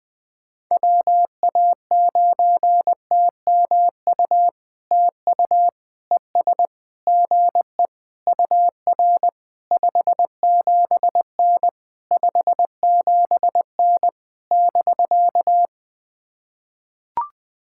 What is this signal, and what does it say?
Morse code, heard as a keyed tone, about 20 wpm, 700 Hz. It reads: WA9TMU TU ES GE UR 57N 57N <BK>
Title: POTA QSO Practice - 20wpm